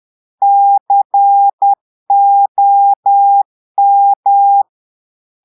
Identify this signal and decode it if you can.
Morse code, heard as a keyed tone, about 10 wpm, 800 Hz. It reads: COM
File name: portugais texte en clair 2 vitesse 10 mots minutes